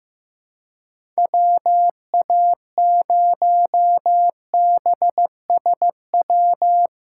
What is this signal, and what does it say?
Morse code, heard as a keyed tone, about 15 wpm, 700 Hz. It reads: WA0BSW
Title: USA Call Signs - 4x Character Spacing - 15wpm